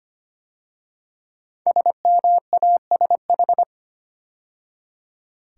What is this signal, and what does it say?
Morse code, heard as a keyed tone, about 25 wpm, 700 Hz. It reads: SMASH
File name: 5 Letter Words - First 3 Letters Given - 25wpm